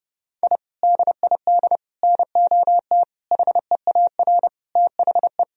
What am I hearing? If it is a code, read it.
Morse code, heard as a keyed tone, about 30 wpm, 700 Hz. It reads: I DID NOT HEAR THE
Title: Sentences from Top 1000 Words - Review Part 2 - 30wpm